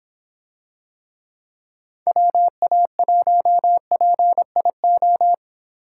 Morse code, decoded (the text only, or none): WA1PIO